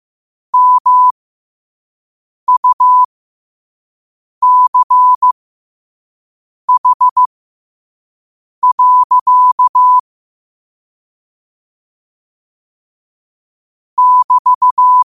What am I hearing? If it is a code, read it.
Morse code, heard as a keyed tone, about 15 wpm, spaced as for 5 wpm, 1000 Hz. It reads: MUCH. <BT>
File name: qotdfiles / 5wpm